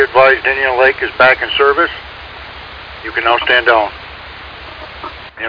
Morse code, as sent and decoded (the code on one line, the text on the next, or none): none